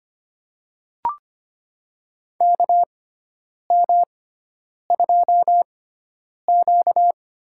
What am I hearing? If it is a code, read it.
Morse code, heard as a keyed tone, about 25 wpm, spaced as for 9 wpm, 700 Hz. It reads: KM2Q
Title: USA Call Signs - 6x Character Spacing - 25wpm